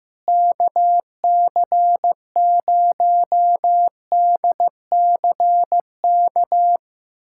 Morse code, heard as a keyed tone, about 16 wpm, 700 Hz. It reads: KC0DCK
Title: USA Callsign DE Callsign - 6 Characters - 15wpm